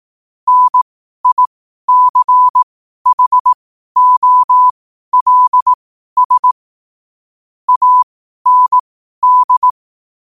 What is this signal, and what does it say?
Morse code, heard as a keyed tone, about 18 wpm, spaced as for 12 wpm, 1000 Hz. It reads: NICHOLS AND
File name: qotdfiles / 12wpm